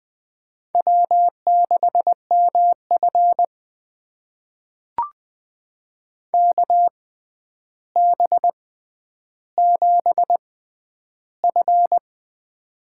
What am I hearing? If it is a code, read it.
Morse code, heard as a keyed tone, about 20 wpm, spaced as for 10 wpm, 700 Hz. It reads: W6MF KB7F
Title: USA Call Signs - 6x Character Spacing - 20wpm